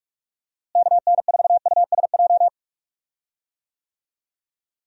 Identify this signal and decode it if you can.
Morse code, heard as a keyed tone, about 44 wpm, 700 Hz. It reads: KN4USJ
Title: USA Call Signs - 6 Characters - Repeated 3x - 45wpm